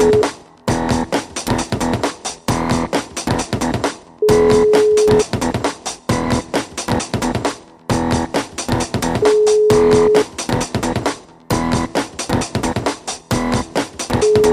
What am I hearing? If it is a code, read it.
Ringback tone (Europe, 425 Hz), with noise.